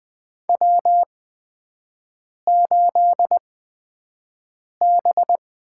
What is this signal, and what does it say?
Morse code, heard as a keyed tone, about 20 wpm, spaced as for 5 wpm, 700 Hz. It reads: W8B
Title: USA Call Signs - 8x Character Spacing - 20wpm